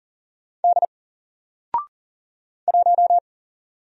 Morse code, heard as a keyed tone, about 40 wpm, 700 Hz. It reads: D 1
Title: Single Letter-Number Alt 40wpm